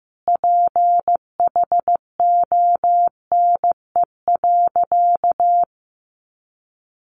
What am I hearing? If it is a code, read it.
Morse code, heard as a keyed tone, about 15 wpm, 700 Hz. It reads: PHONE.
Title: Sentences from Top 1100 Words 15wpm